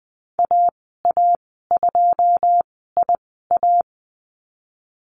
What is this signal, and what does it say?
Morse code, heard as a keyed tone, about 20 wpm, spaced as for 15 wpm, 700 Hz. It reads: AA2IA
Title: 088 Lesson 15g - Number 5 - Call Signs 20wpm